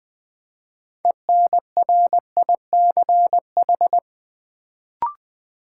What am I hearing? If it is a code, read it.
Morse code, heard as a keyed tone, about 20 wpm, 700 Hz. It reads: ENRICH
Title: Build A Word - 2 Letters At A Time - 6-Letter Words - 20wpm